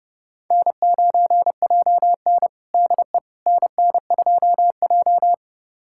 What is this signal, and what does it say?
Morse code, heard as a keyed tone, about 31 wpm, 700 Hz. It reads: N9JN DE NN2J
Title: USA Callsign DE Callsign - 4 Characters - 30wpm